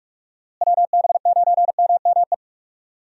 Morse code, heard as a keyed tone, about 42 wpm, 700 Hz. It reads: WB9MME
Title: USA Call Signs - 6 Characters - Repeated 3x - 45wpm